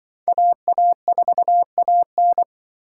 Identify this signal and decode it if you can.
Morse code, heard as a keyed tone, about 25 wpm, 700 Hz. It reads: AA4AN
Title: Speed Racing - USA 2x2 Call Signs 20wpm